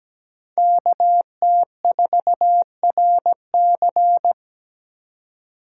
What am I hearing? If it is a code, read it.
Morse code, heard as a keyed tone, about 20 wpm, 700 Hz. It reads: KT4RC
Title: Speed Racing - USA Call Signs 17wpm